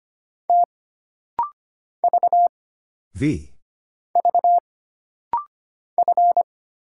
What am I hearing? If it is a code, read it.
Morse code, heard as a keyed tone, about 25 wpm, 700 Hz. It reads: T V V F
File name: Single Character (Letter:Number:Punctuation:Prosign) - Rapid-Fire - 25wpm